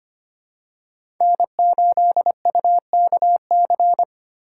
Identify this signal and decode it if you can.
Morse code, heard as a keyed tone, about 25 wpm, 700 Hz. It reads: N8UKC
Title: USA Call Signs - 5 Characters - 6x Character Spacing - 25wpm